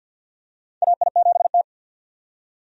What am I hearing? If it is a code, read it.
Morse code, heard as a keyed tone, about 50 wpm, 700 Hz. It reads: AI7T
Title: USA Call Signs - 4 Characters - 50wpm